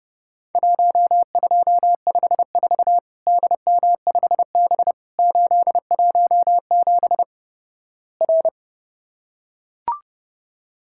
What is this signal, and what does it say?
Morse code, heard as a keyed tone, about 32 wpm, 700 Hz. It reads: 1254 DM5B 817 R
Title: Worked All Europe DX Contest - QTC Exchange - 30wpm